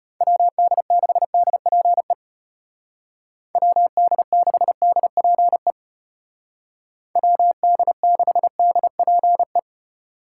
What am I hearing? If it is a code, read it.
Morse code, heard as a keyed tone, about 32 wpm, 700 Hz. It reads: WD6DPE WD6DPE WD6DPE